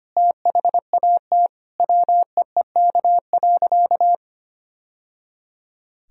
Morse code, heard as a keyed tone, about 25 wpm, 700 Hz. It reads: THAT WEEK.
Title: Sentences from Top 1000 Words - Review Part 2 - 25wpm